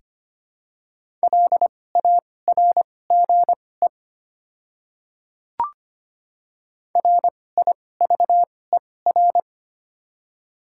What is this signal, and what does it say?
Morse code, heard as a keyed tone, about 25 wpm, spaced as for 18 wpm, 700 Hz. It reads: LARGE RIVER
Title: 2x Character Spacing - Top 500 Words - 25wpm